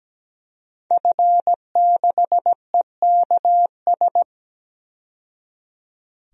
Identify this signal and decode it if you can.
Morse code, heard as a keyed tone, about 17 wpm, 700 Hz. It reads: F6EKS
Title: DXCC-100 Call Signs - 5 Characters - 17wpm